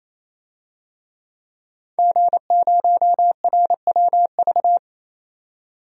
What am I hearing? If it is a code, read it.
Morse code, heard as a keyed tone, about 28 wpm, 700 Hz. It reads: G0RWV
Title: DXCC-100 Call Signs - 5 Characters - 28wpm